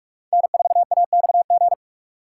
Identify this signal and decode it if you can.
Morse code, heard as a keyed tone, about 47 wpm, 700 Hz. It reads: N4AXG
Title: USA Call Signs - 5 Characters - 4x Character Spacing - 45wpm